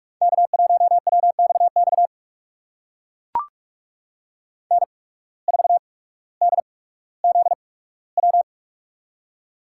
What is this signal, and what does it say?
Morse code, heard as a keyed tone, about 44 wpm, spaced as for 19 wpm, 700 Hz. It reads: K1WXX N4DZW
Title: USA Call Signs - 5 Characters - 8x Character Spacing - 45wpm